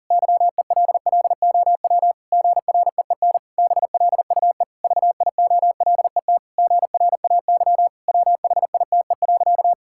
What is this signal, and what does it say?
Morse code, heard as a keyed tone, about 40 wpm, 700 Hz. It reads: YELLOW GREEN BLUE VIOLET GRAY WHITE.
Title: Pangram Challenge v2 - 40wpm